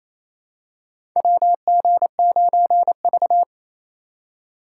Morse code, heard as a keyed tone, about 27 wpm, 700 Hz. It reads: WG9V
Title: USA Call Signs - 3x Character Spacing - 28wpm